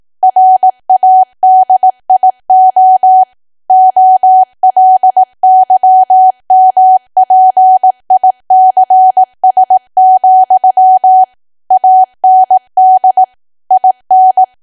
Morse code, heard as a keyed tone, about 18 wpm, 750 Hz. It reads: RADIO OLYMPICS, AND IN
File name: Archive / 18 WPM / 18WPM